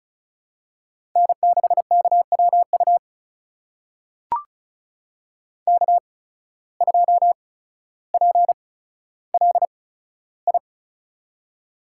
Morse code, heard as a keyed tone, about 36 wpm, spaced as for 15 wpm, 700 Hz. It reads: N6KWU K2PLI
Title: USA Call Signs - 5 Characters - 8x Character Spacing - 35wpm